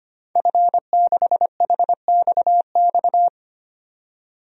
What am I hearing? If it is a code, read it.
Morse code, heard as a keyed tone, about 25 wpm, 700 Hz. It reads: F6HXX
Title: DXCC-100 Call Signs - 5 Characters - 25wpm